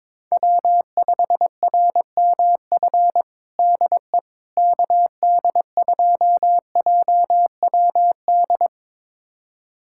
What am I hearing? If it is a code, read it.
Morse code, heard as a keyed tone, about 22 wpm, 700 Hz. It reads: W5RMF DE KD2JWD
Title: USA Callsign DE Callsign 22wpm